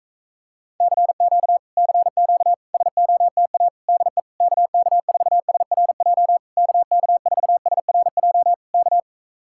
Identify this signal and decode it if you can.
Morse code, heard as a keyed tone, about 45 wpm, 700 Hz. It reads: CQ CQ SOTA DE KK4SRJ KK4SRJ K